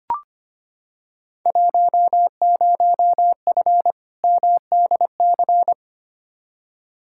Morse code, heard as a keyed tone, about 25 wpm, 700 Hz. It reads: 10F MDC
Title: Field Day Exchanges Encore - 25wpm